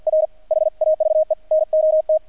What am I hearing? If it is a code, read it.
Morse code, heard as a keyed tone, about 48 wpm, 630 Hz. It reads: R HAVE NOT
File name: archived / CRSnet